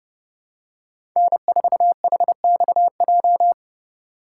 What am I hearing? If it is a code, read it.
Morse code, heard as a keyed tone, about 30 wpm, 700 Hz. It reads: N4HXJ
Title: USA Call Signs - 5 Characters - 3x Character Spacing - 30wpm